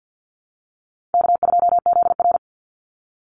Morse code, heard as a keyed tone, about 50 wpm, 700 Hz. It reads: K2ZR
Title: Easy Call Signs - 50wpm